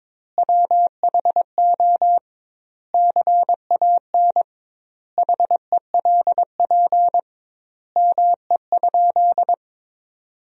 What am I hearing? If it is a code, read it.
Morse code, heard as a keyed tone, about 21 wpm, 700 Hz. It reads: WHO CAN HELP ME?
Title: 2x Word-Spacing - Sentences from Top 200 Words - 22wpm